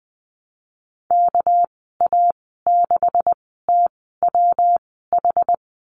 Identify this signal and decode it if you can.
Morse code, heard as a keyed tone, about 20 wpm, spaced as for 15 wpm, 700 Hz. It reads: KA6TWH
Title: 200 Lesson 32g - Letter k - Call Signs 20wpm